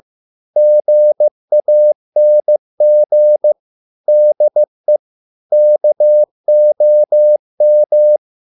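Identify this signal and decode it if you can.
Morse code, heard as a keyed tone, about 15 wpm, 600 Hz. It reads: GANG DE KOM
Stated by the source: Morse code at a slow 15 wpm from CW med Gnister.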